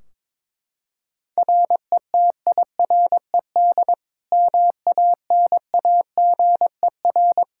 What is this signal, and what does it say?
Morse code, heard as a keyed tone, about 22 wpm, 700 Hz. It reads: RETIRED MANAGER
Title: Practice QSO - Exchange 3 - More Info - 22wpm